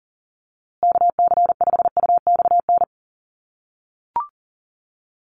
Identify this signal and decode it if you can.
Morse code, heard as a keyed tone, about 40 wpm, 700 Hz. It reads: KC5UXN
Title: USA Call Signs Repeated 2x 40wpm